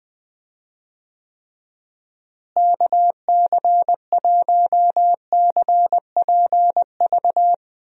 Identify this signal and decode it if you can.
Morse code, heard as a keyed tone, about 20 wpm, 700 Hz. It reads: KC1CPV
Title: USA Call Signs - 6 Characters - Repeated 3x - 20wpm